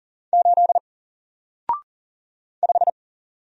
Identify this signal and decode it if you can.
Morse code, heard as a keyed tone, about 40 wpm, 700 Hz. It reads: 8 5
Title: Single Character (Letter:Number:Punctuation:Prosign) Alt 40wpm